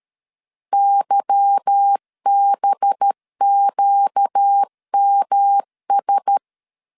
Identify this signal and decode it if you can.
Morse code, heard as a keyed tone, about 13 wpm, 800 Hz. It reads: YBQMS
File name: LEZIONI / 8 YJ45 / 8 YJ4 12wpm 2151